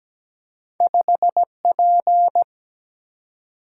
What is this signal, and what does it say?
Morse code, heard as a keyed tone, about 17 wpm, 700 Hz. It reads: 5P
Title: Single-Letter Single-Number 17wpm